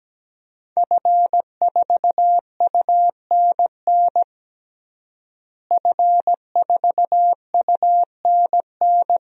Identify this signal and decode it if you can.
Morse code, heard as a keyed tone, about 17 wpm, 700 Hz. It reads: F4UNN F4UNN
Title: DXCC-100 Call Signs - 5 Characters - Repeated 2x - 17wpm